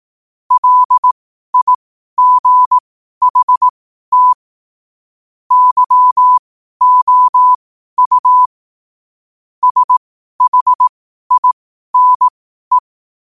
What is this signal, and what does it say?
Morse code, heard as a keyed tone, about 18 wpm, spaced as for 12 wpm, 1000 Hz. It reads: LIGHT YOU SHINE
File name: qotdfiles / 12wpm